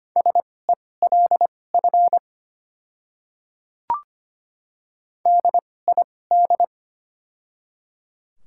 Morse code, heard as a keyed tone, about 25 wpm, spaced as for 18 wpm, 700 Hz. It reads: SELF DID
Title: 2x Character Spacing - Top 200 Words - 25wpm